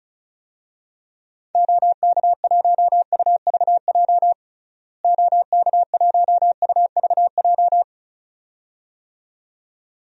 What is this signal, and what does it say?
Morse code, heard as a keyed tone, about 34 wpm, 700 Hz. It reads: OK1UVJ OK1UVJ
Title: DXCC-100 Call Signs - 6 Characters - Repeated 2x - 35wpm